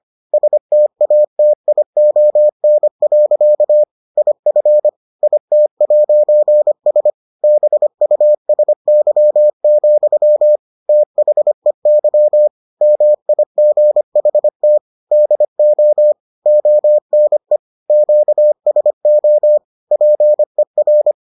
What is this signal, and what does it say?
Morse code, heard as a keyed tone, about 25 wpm, 600 Hz. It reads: STATION. IF IT'S BUSY, THEY MIGHT DO ONE QSO PER